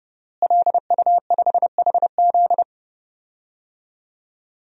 Morse code, heard as a keyed tone, about 30 wpm, 700 Hz. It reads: LU5HZ
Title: DXCC-100 Call Signs - 5 Characters - Repeated 2x - 30wpm